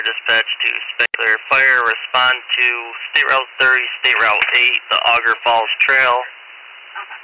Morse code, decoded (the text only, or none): none